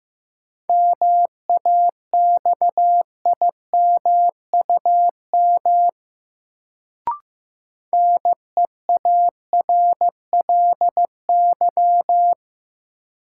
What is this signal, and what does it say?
Morse code, heard as a keyed tone, about 15 wpm, 700 Hz. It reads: MAXIMUM NEARLY
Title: Sentences from Top 1300 Words - New Word Drill Alt - 15wpm